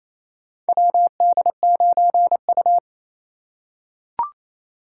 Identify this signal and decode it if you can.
Morse code, heard as a keyed tone, about 28 wpm, 700 Hz. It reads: WD9U
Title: USA Call Signs - 2x Character Spacing - 28wpm